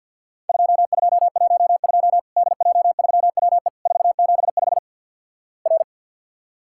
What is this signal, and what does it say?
Morse code, heard as a keyed tone, about 50 wpm, 700 Hz. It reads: 2212 DJ3WE 475 R
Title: Worked All Europe DX Contest - QTC Exchange - 50wpm